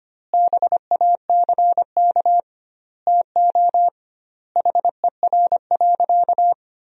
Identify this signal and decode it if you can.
Morse code, heard as a keyed tone, about 25 wpm, 700 Hz. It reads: BACK TO HER.
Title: 2x Word-Spacing - Sentences from Top 300 Words - 25wpm